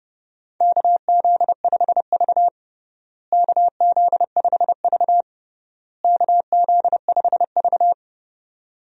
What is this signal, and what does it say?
Morse code, heard as a keyed tone, about 32 wpm, 700 Hz. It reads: KZ5V KZ5V KZ5V
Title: USA Call Signs - 4 Characters - Repeated 3x - 30wpm